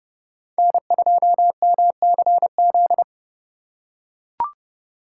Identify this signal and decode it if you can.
Morse code, heard as a keyed tone, about 26 wpm, 700 Hz. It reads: N2MCZ